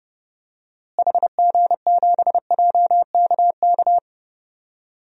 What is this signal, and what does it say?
Morse code, heard as a keyed tone, about 32 wpm, 700 Hz. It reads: HG7JKK
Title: DXCC-100 Call Signs - 6 Characters - Repeated 3x - 30wpm